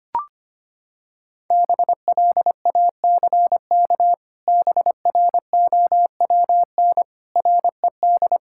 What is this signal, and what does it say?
Morse code, heard as a keyed tone, about 25 wpm, 700 Hz. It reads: BLACK BROWN RED
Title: Pangram Challenge v2 - 25wpm